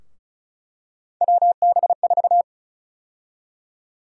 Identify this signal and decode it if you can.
Morse code, heard as a keyed tone, about 36 wpm, 700 Hz. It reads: WB4
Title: USA Call Sign Prefixes - Repeated 2x - 35wpm